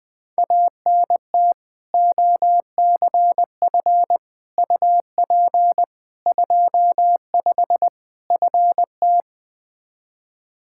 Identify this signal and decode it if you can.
Morse code, heard as a keyed tone, about 20 wpm, 700 Hz. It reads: ANT OCF UP 25 FT
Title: Practice QSO - Exchange 2 - Info - 20wpm